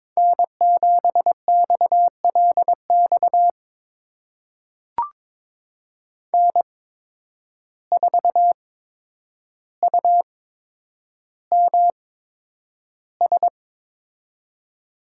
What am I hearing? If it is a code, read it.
Morse code, heard as a keyed tone, about 22 wpm, spaced as for 9 wpm, 700 Hz. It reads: N7XLX N4UMS
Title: USA Call Signs - 5 Characters - 8x Character Spacing - 22wpm